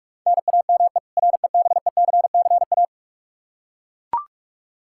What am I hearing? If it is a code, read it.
Morse code, heard as a keyed tone, about 41 wpm, 700 Hz. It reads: NAME REBECCA
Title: K1USN SST QSO Practice 45wpm